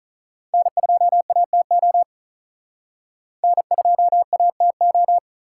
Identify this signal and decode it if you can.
Morse code, heard as a keyed tone, about 39 wpm, 700 Hz. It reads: N2ATO N2ATO
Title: Speed Racing - USA 1x3 Call Signs 35wpm